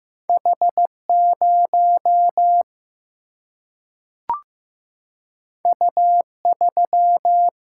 Morse code, heard as a keyed tone, about 15 wpm, 700 Hz. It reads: H0 U3
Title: Single-Letter Single-Number 15wpm